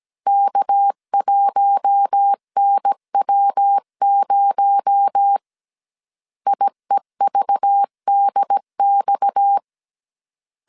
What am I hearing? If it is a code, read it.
Morse code, heard as a keyed tone, about 17 wpm, 800 Hz. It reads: K1NW0 IEVDX
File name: 16wpm